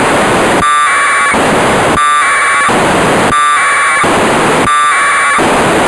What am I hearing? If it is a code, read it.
Packet radio (AX.25, as in APRS).
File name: Negai 1200bd packet received by JA6PL provided by JA0CAW